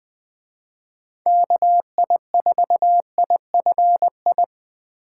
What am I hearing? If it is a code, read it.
Morse code, heard as a keyed tone, about 20 wpm, 700 Hz. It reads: KI4IFI